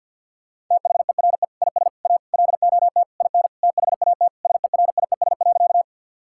Morse code, heard as a keyed tone, about 51 wpm, 700 Hz. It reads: THERE IS A LOT IN THAT SERIES.
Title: Sentences from Top 1000 Words - Review Part 5 - 50wpm